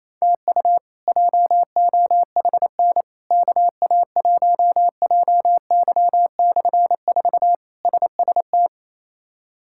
Morse code, heard as a keyed tone, about 28 wpm, 700 Hz. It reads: TU JOHN KA1JY/4 SST
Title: K1USN SST QSO Practice - 28wpm